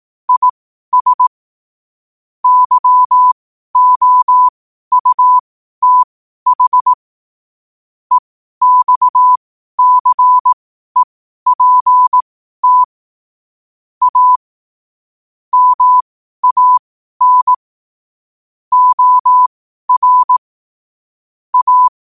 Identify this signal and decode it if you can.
Morse code, heard as a keyed tone, about 18 wpm, spaced as for 12 wpm, 1000 Hz. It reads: IS YOUTH EXCEPT A MAN OR A
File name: qotdfiles / 12wpm